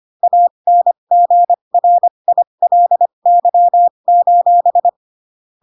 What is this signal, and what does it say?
Morse code, heard as a keyed tone, about 25 wpm, spaced as for 22 wpm, 700 Hz. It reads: ANGRILY:
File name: Pinocchio / Pinocchio ch3 18 25 ik1hge